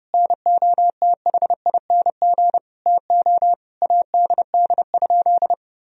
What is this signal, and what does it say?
Morse code, heard as a keyed tone, about 32 wpm, 700 Hz. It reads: NOTHING TO ADD?
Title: Sentences from Top 1000 Words - Review Part 1 - 30wpm